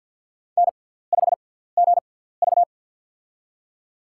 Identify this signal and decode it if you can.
Morse code, heard as a keyed tone, about 48 wpm, spaced as for 17 wpm, 690 Hz. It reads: N5GV
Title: USA Call Signs - 6x Character Spacing - 50wpm